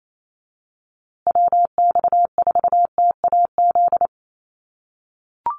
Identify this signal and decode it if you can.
Morse code, heard as a keyed tone, about 28 wpm, 700 Hz. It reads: WX4TAZ
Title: USA Call Signs Repeated 2x 28wpm